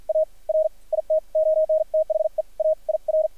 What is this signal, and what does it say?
Morse code, heard as a keyed tone, about 53 wpm, spaced as for 40 wpm, 620 Hz, with radio noise. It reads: AR IT ON THE AIR
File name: archived / CRSnet